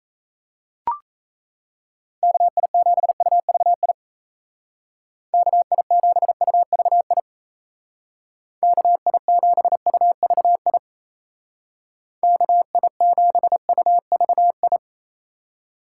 Morse code, heard as a keyed tone, about 34 wpm, 700 Hz. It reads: KI7UVI KI7UVI KI7UVI KI7UVI
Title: Speed Racing - USA Call Signs 28wpm